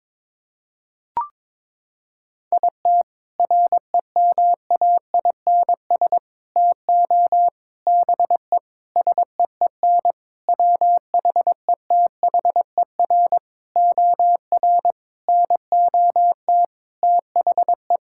Morse code, heard as a keyed tone, about 22 wpm, 700 Hz. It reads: IT REMAINS TO BE SEEN WHETHER OR NOT THE
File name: practice / 22wpm / Sentences with 8-Word Phrases - 22wpm